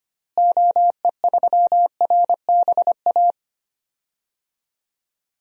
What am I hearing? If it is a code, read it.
Morse code, heard as a keyed tone, about 25 wpm, 700 Hz. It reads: OE3RBA
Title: DXCC-100 Call Signs - 6 Characters - Repeated 3x - 25wpm